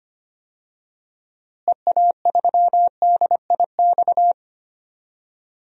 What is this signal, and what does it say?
Morse code, heard as a keyed tone, about 25 wpm, 700 Hz. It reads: EA3DIX
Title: DXCC-100 Call Signs - 6 Characters - Repeated 3x - 25wpm